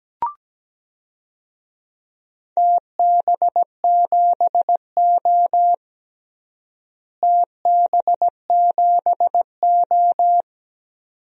Morse code, heard as a keyed tone, about 17 wpm, 700 Hz. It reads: TB7O TB7O
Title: DXCC-100 Call Signs - 4 Characters - Repeated 2x - 17wpm